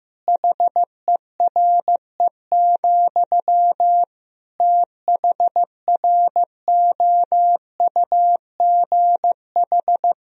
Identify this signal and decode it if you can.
Morse code, heard as a keyed tone, about 16 wpm, 700 Hz. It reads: HERE, THROUGH
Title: Sentences from Top 1000 Words - Review Part 1 - 15wpm